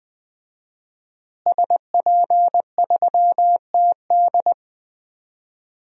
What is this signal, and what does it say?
Morse code, heard as a keyed tone, about 20 wpm, 700 Hz. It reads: SP3TD
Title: DXCC-100 Call Signs - 5 Characters - Repeated 3x - 20wpm